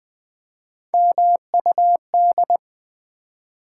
Morse code, heard as a keyed tone, about 20 wpm, 700 Hz. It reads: MUD
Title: Three Letter Words - 20wpm